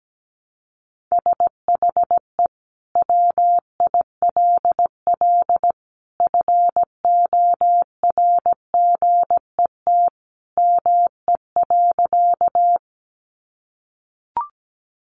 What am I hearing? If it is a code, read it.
Morse code, heard as a keyed tone, about 17 wpm, 700 Hz. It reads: SHE WILL FORGET ME.
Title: Sentences from Top 1200 Words 17wpm